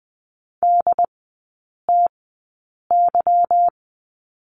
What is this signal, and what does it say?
Morse code, heard as a keyed tone, about 20 wpm, spaced as for 8 wpm, 690 Hz. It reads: DTY
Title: 145 Lesson 24e - Letter y - Three Characters 20wpm